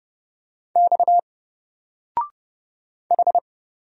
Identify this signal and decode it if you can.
Morse code, heard as a keyed tone, about 30 wpm, 700 Hz. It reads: X H
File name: Single-Letter-Number-30wpm